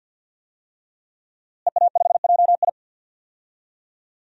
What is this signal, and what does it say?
Morse code, heard as a keyed tone, about 52 wpm, 700 Hz. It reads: EA5JI
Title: DXCC-100 Call Signs - 5 Characters - Repeated 3x - 50wpm